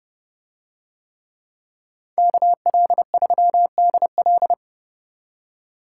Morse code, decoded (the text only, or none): KL3DL